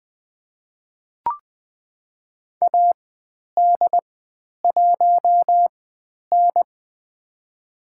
Morse code, heard as a keyed tone, about 20 wpm, spaced as for 10 wpm, 700 Hz. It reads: AD1N
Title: 095 Lesson 16g - Letter c - Call Signs 20wpm